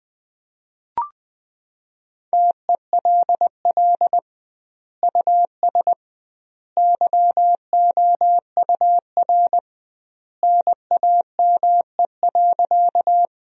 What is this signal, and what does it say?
Morse code, heard as a keyed tone, about 21 wpm, 700 Hz. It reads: TELL US YOUR NAME.
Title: 2x Word-Spacing - Sentences from Top 200 Words - 20wpm